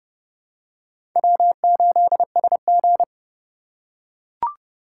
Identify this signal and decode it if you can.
Morse code, heard as a keyed tone, about 29 wpm, 700 Hz. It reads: W8SG